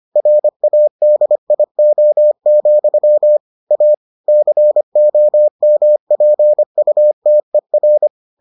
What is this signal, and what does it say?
Morse code, heard as a keyed tone, about 25 wpm, 600 Hz. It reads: RADIO, A COMPUTER